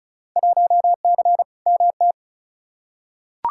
Morse code, heard as a keyed tone, about 36 wpm, 700 Hz. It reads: 1C MT
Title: Field Day Exchanges Encore - 35wpm